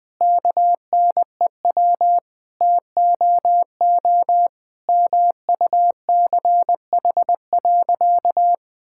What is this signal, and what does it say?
Morse code, heard as a keyed tone, about 20 wpm, 700 Hz. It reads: KNEW TOO MUCH.
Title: Sentences from Top 1000 Words - Review Part 5 - 20wpm